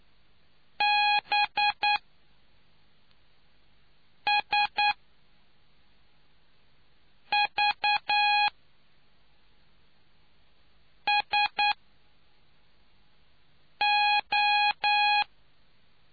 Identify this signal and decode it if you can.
Morse code, heard as a keyed tone, about 9 wpm, spaced as for 3 wpm, 830 Hz. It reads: BSVSO